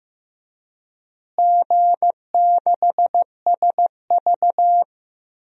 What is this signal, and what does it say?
Morse code, heard as a keyed tone, about 15 wpm, 700 Hz. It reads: G6SV
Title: DXCC-100 Call Signs - 4 Characters - Repeated 2x - 15wpm